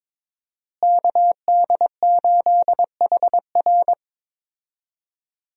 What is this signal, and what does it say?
Morse code, heard as a keyed tone, about 22 wpm, 700 Hz. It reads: KD8HR
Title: USA Call Signs - 5 Characters - Repeated 3x - 22wpm